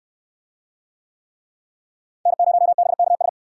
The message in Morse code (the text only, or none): N0BKS